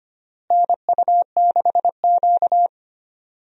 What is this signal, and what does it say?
Morse code, heard as a keyed tone, about 25 wpm, 700 Hz. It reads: NU6Q
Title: USA Call Signs - 4 Characters - Repeated 3x - 25wpm